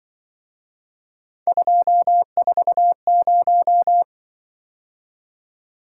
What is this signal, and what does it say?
Morse code, heard as a keyed tone, about 23 wpm, 700 Hz. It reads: 240
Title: Speed-Racing - Three Letter-Number Combinations - 20wpm